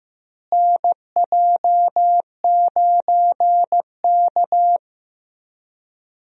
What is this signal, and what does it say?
Morse code, heard as a keyed tone, about 15 wpm, 700 Hz. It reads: NJ9K
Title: USA Call Signs - 4 Characters - Repeated 2x - 15wpm